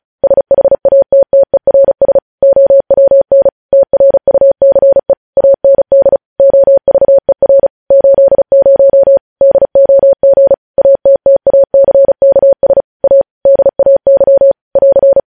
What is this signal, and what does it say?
Morse code, heard as a keyed tone, about 36 wpm, 550 Hz. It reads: SHATTERS OWN TRUCE AND OVER 80 DOG ATTACKS A DAY <AR>
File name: podcast / News headlines-1x-35x35